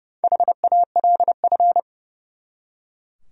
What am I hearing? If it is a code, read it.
Morse code, heard as a keyed tone, about 38 wpm, 700 Hz. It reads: HALF